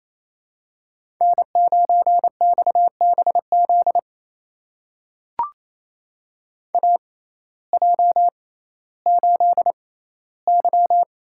Morse code, heard as a keyed tone, about 28 wpm, spaced as for 15 wpm, 700 Hz. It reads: N9XBZ AJ8Y
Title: USA Call Signs - 6x Character Spacing - 28wpm